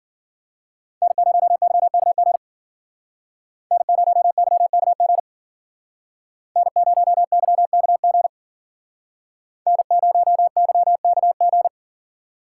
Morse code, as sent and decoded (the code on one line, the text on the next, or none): -. ----- -.-- -.- --. / -. ----- -.-- -.- --. / -. ----- -.-- -.- --. / -. ----- -.-- -.- --.
N0YKG N0YKG N0YKG N0YKG